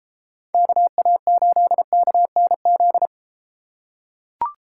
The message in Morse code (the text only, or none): KA8KNZ